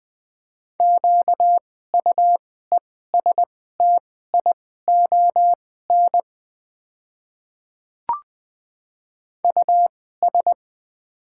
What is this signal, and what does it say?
Morse code, heard as a keyed tone, about 20 wpm, spaced as for 14 wpm, 690 Hz. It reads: QUESTION US
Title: 2x Character Spacing - Top 500 Words - 20wpm